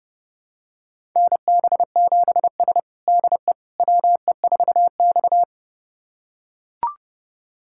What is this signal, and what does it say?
Morse code, heard as a keyed tone, about 32 wpm, 700 Hz. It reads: NB7S DE WE4X
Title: USA Callsign DE Callsign - 4 Characters - 30wpm